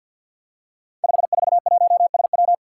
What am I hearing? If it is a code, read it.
Morse code, heard as a keyed tone, about 52 wpm, 700 Hz. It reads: 541SW
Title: Five Challenging Letter-Number Combinations 50wpm